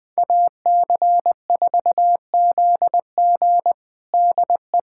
Morse code, heard as a keyed tone, about 21 wpm, 700 Hz. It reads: AC4ZG DE
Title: USA Callsign DE Callsign - 5 Characters - 20wpm